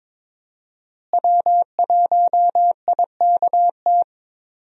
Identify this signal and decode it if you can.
Morse code, heard as a keyed tone, about 22 wpm, 700 Hz. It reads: W1IKT
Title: USA Call Signs - 5 Characters - Repeated 2x - 22wpm